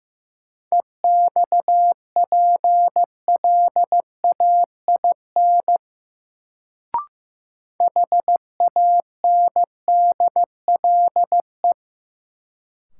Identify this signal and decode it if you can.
Morse code, heard as a keyed tone, about 15 wpm, 700 Hz. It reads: EXPLAIN HANDLE
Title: Sentences from Top 1100 Words - New Word Drill - 15wpm